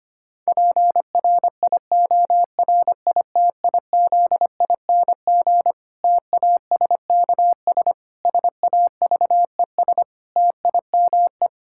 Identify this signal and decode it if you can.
Morse code, heard as a keyed tone, about 25 wpm, 700 Hz. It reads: PRIORITIZING TASKS SAVES TIME
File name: practice / 25wpm / Difficult Sentences with Hints - 25wpm